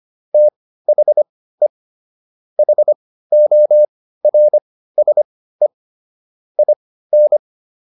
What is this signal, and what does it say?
Morse code, heard as a keyed tone, about 25 wpm, spaced as for 15 wpm, 600 Hz. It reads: THE HORSE IN